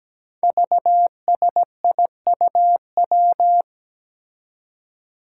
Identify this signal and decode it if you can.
Morse code, heard as a keyed tone, about 17 wpm, 700 Hz. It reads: VSIUW
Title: Five Challenging Letter-Number Combinations 17wpm